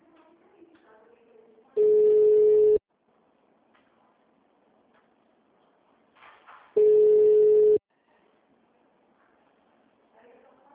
Ringback tone (Europe, 425 Hz).
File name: out- -2004- - - .140